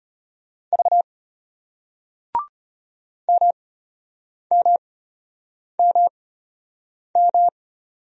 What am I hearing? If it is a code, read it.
Morse code, heard as a keyed tone, about 33 wpm, spaced as for 8 wpm, 700 Hz. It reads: V MMMM